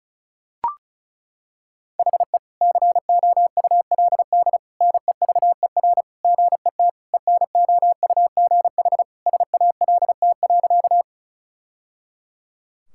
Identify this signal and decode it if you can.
Morse code, heard as a keyed tone, about 36 wpm, 700 Hz. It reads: HE COULD NEVER GET ENOUGH SALT.
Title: Sentences from Top 1000 Words - Review Part 5 - 35wpm